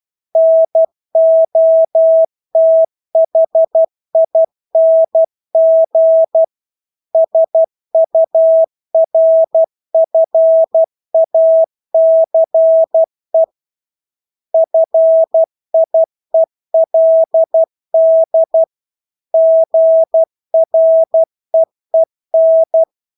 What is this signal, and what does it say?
Morse code, heard as a keyed tone, about 12 wpm, 650 Hz. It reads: NOTHING SURFACE FIELD GREEN
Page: Engelske ord 12 wpm